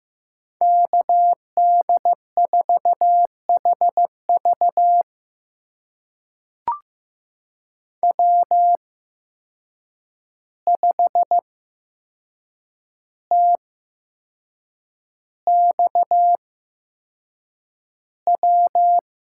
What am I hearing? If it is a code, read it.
Morse code, heard as a keyed tone, about 15 wpm, spaced as for 6 wpm, 700 Hz. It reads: KD4HV W5TXW
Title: USA Call Signs - 5 Characters - 8x Character Spacing - 15wpm